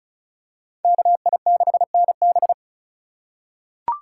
Morse code, heard as a keyed tone, about 34 wpm, 700 Hz. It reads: KI6NB
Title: USA Call Signs - 5 Characters - 4x Character Spacing - 35wpm